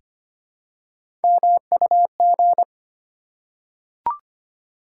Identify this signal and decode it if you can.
Morse code, heard as a keyed tone, about 25 wpm, 700 Hz. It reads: MUG
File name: practice / 25wpm / Easy Words with Hints - 25wpm